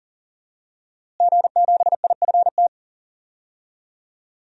Morse code, heard as a keyed tone, about 37 wpm, 700 Hz. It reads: G7IFT
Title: DXCC-100 Call Signs - 5 Characters - Repeated 3x - 40wpm